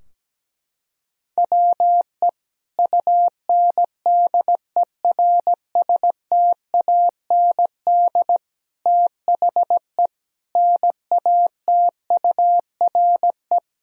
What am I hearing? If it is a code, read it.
Morse code, heard as a keyed tone, about 17 wpm, 700 Hz. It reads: WE UNDERSTAND THE NATURE
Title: Sentences from Top 1000 Words - Review Part 2 - 17wpm